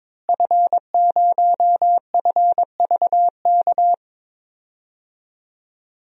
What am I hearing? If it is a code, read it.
Morse code, heard as a keyed tone, about 22 wpm, 700 Hz. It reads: F0FVK
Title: DXCC-100 Call Signs - 5 Characters - Repeated 3x - 22wpm